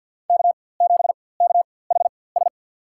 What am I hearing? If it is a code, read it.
Morse code, heard as a keyed tone, about 53 wpm, spaced as for 24 wpm, 700 Hz. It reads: K7KHS
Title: USA Call Signs - 5 Characters - 4x Character Spacing - 50wpm